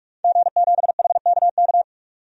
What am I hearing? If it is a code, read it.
Morse code, heard as a keyed tone, about 43 wpm, 700 Hz. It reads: G7HKK
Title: DXCC-100 Call Signs - 5 Characters - 45wpm